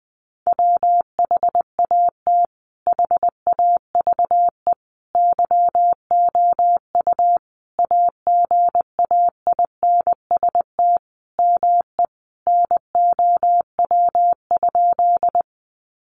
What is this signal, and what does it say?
Morse code, heard as a keyed tone, about 20 wpm, 700 Hz. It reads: WHAT HAVE YOU AGAINST ME NOW?
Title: Sentences from Top 400 Words 20wpm